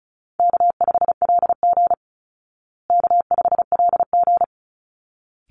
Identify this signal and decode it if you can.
Morse code, heard as a keyed tone, about 34 wpm, 700 Hz. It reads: K5LG K5LG
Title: Easy Call Signs Repeated 2x - 35wpm